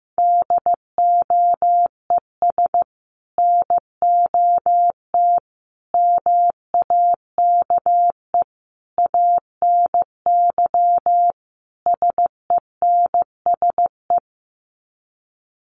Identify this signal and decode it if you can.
Morse code, heard as a keyed tone, about 15 wpm, 700 Hz. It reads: DOES NOT MAKE ANY SENSE
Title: Sets of 5 Words Encore2 15wpm